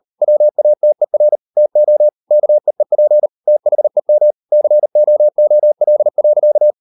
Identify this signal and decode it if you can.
Morse code, heard as a keyed tone, about 38 wpm, 600 Hz. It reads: WATER TO KEEP THEM COOL.